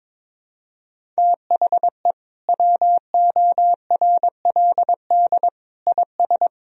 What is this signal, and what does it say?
Morse code, heard as a keyed tone, about 22 wpm, 700 Hz. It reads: THE WORLD IS
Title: Sentences from Top 200 Words and Abbreviations 22wpm